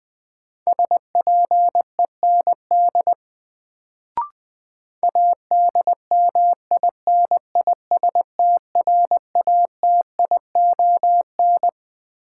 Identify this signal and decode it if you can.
Morse code, heard as a keyed tone, about 21 wpm, 700 Hz. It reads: SPEND ADMINISTRATION
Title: Sets of 5 Words - New Word Drill - 20wpm